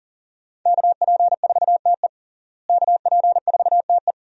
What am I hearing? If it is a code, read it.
Morse code, heard as a keyed tone, about 40 wpm, 700 Hz. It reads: KP4TE KP4TE
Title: DXCC-100 Call Signs - 5 Characters - Repeated 2x - 40wpm